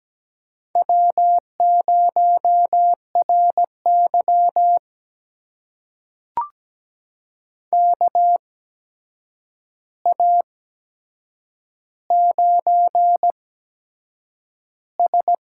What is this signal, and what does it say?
Morse code, heard as a keyed tone, about 17 wpm, spaced as for 7 wpm, 700 Hz. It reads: W0RY KA9S
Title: USA Call Signs - 8x Character Spacing - 17wpm